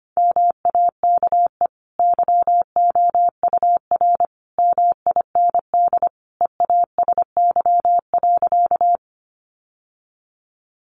Morse code, heard as a keyed tone, about 25 wpm, 700 Hz. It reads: MAKE YOUR MIND EASY.
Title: Sentences from Top 700 Words 25wpm